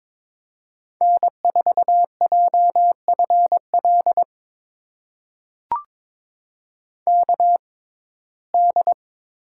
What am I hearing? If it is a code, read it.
Morse code, heard as a keyed tone, about 22 wpm, spaced as for 16 wpm, 700 Hz. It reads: N4JFL KD
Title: USA Call Signs - 6x Character Spacing - 22wpm